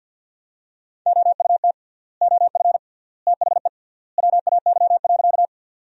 Morse code, heard as a keyed tone, about 50 wpm, 700 Hz. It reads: OUT OF THE WAY.
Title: 3x Word-Spacing - Sentences from Top 100 Words - 50wpm